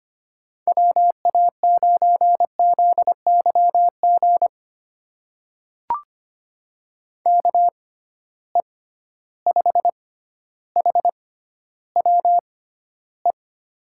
Morse code, heard as a keyed tone, about 25 wpm, spaced as for 13 wpm, 700 Hz. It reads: WA9ZYG KE5HWE
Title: USA Call Signs - 6 Characters - 6x Character Spacing - 25wpm